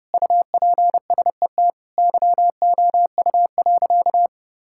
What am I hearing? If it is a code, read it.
Morse code, heard as a keyed tone, about 32 wpm, 700 Hz. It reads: UPSET YOU.
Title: Sentences from Top 1000 Words - Review Part 5 - 30wpm